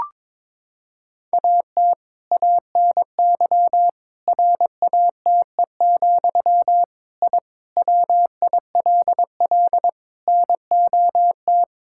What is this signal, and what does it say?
Morse code, heard as a keyed tone, about 22 wpm, 700 Hz. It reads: AT ANY RATE, I WILL NOT
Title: Sentences from Top 1000 Words - Review Part 6 - 22wpm